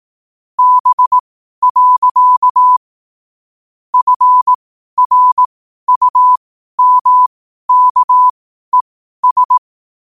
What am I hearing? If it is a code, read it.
Morse code, heard as a keyed tone, about 18 wpm, spaced as for 12 wpm, 1000 Hz. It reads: B. FRUMKES
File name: qotdfiles / 12wpm